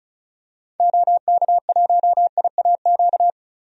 Morse code, heard as a keyed tone, about 36 wpm, 700 Hz. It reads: OK1IAQ
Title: DXCC-100 Call Signs - 6 Characters - 35wpm